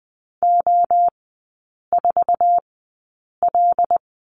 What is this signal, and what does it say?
Morse code, heard as a keyed tone, about 20 wpm, spaced as for 8 wpm, 700 Hz. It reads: O4L
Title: 218 Lesson 35e - Number 0 - Three Characters 20wpm